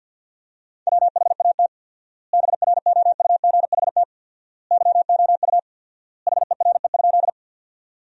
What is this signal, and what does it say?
Morse code, heard as a keyed tone, about 50 wpm, 700 Hz. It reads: WHAT BROUGHT YOU HERE?
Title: 4x Word-Spacing - Sentences from Top 500 Words - 50wpm